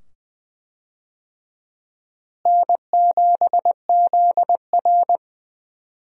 Morse code, heard as a keyed tone, about 20 wpm, 700 Hz. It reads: N7ZR
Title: USA Call Signs - 4 Characters - Repeated 2x - 20wpm